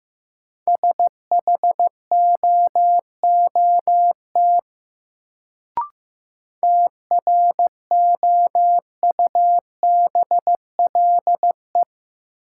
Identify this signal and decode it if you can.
Morse code, heard as a keyed tone, about 15 wpm, 700 Hz. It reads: SHOOT TROUBLE
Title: Sentences from Top 1200 Words - New Word Drill - 15wpm